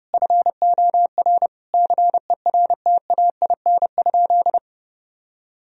Morse code, heard as a keyed tone, about 32 wpm, 700 Hz. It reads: FOR CERTAIN?
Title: Sentences from Top 1000 Words - Review Part 2 - 30wpm